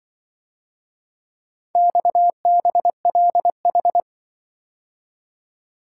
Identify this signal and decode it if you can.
Morse code, heard as a keyed tone, about 25 wpm, 700 Hz. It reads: XBLH
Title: Speed-Racing - Four Letter-Number Combinations - 20wpm